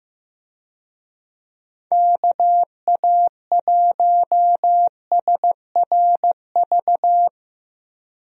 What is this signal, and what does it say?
Morse code, heard as a keyed tone, about 15 wpm, 700 Hz. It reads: KA1SRV
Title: USA Call Signs - 6 Characters - Repeated 3x - 15wpm